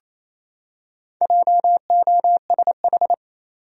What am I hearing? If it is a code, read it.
Morse code, heard as a keyed tone, about 28 wpm, 700 Hz. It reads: JOSH
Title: USA Common Names - 28wpm